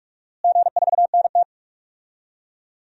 Morse code, heard as a keyed tone, about 42 wpm, 700 Hz. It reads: G4NT
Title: DXCC-100 Call Signs - 4 Characters - Repeated 2x - 45wpm